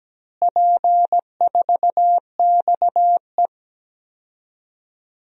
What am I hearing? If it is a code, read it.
Morse code, heard as a keyed tone, about 17 wpm, 700 Hz. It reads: P4XE
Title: DXCC-100 Call Signs - 4 Characters - Repeated 3x - 17wpm